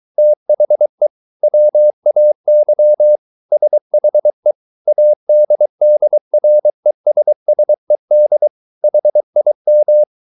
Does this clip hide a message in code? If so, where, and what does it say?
Yes: Morse code from 0.2 to 10.0 s, heard as a keyed tone, about 23 wpm, 600 Hz. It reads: THE WAY SHE ADDRESSED HIM